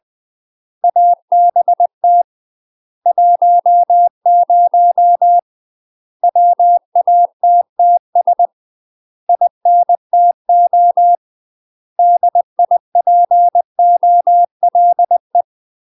Morse code, heard as a keyed tone, about 20 wpm, 700 Hz. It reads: ABT 10 WATTS INTO DIPOLE
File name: UK MASTER QSOs TRAINING / 20WPM / UK MASTER QSOs 0070